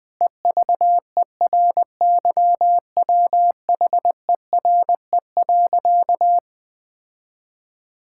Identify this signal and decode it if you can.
Morse code, heard as a keyed tone, about 19 wpm, 700 Hz. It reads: EVERYWHERE.
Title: Sentences from Top 1600 Words 20wpm